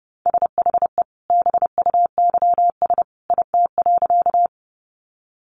Morse code, heard as a keyed tone, about 30 wpm, 700 Hz. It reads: SHE BUYS IT.
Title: Additional Easy Phrases 30wpm